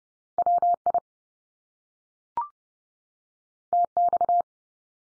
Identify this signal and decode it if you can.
Morse code, heard as a keyed tone, about 30 wpm, 700 Hz. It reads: WI TX
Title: US States 30WPM